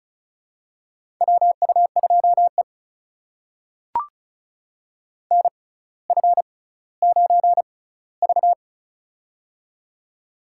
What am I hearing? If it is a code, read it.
Morse code, heard as a keyed tone, about 36 wpm, spaced as for 18 wpm, 700 Hz. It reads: WU2E NF9V
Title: USA Call Signs - 4 Characters - 6x Character Spacing - 35wpm